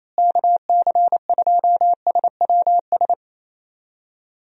Morse code, heard as a keyed tone, about 28 wpm, 700 Hz. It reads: KC2SWS
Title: USA Call Signs - 6 Characters - Repeated 3x - 28wpm